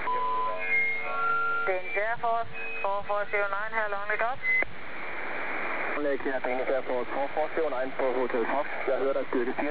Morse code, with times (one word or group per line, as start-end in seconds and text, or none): none